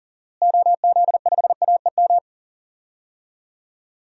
Morse code, heard as a keyed tone, about 37 wpm, 700 Hz. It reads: OZ5AEM
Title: DXCC-100 Call Signs - 6 Characters - Repeated 3x - 40wpm